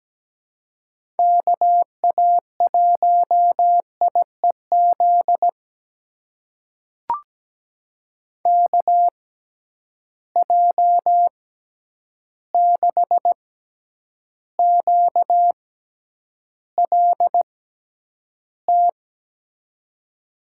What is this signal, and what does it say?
Morse code, heard as a keyed tone, about 17 wpm, spaced as for 9 wpm, 700 Hz. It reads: KA1IEZ KJ6QLT
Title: USA Call Signs - 6 Characters - 6x Character Spacing - 17wpm